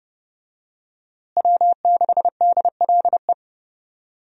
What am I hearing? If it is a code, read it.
Morse code, heard as a keyed tone, about 30 wpm, 700 Hz. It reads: W6DLE
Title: USA Call Signs - 5 Characters - Repeated 3x - 30wpm